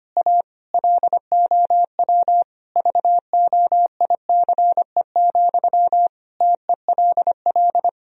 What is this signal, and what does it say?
Morse code, heard as a keyed tone, about 25 wpm, 700 Hz. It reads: A LOW VOICE, TELL
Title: Sentences from Top 1000 Words - Review Part 1 - 25wpm